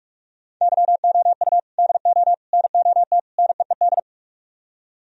Morse code, heard as a keyed tone, about 44 wpm, 700 Hz. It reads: YOU DO NOT NEED